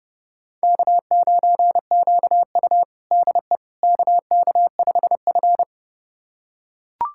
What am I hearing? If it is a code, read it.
Morse code, heard as a keyed tone, about 32 wpm, 700 Hz. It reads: K9QU DE KK5F
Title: USA Callsign DE Callsign - 4 Characters - 30wpm